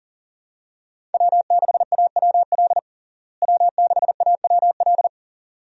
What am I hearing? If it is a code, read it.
Morse code, heard as a keyed tone, about 40 wpm, 700 Hz. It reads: W6AWL W6AWL
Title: USA Call Signs - 5 Characters - Repeated 2x - 40wpm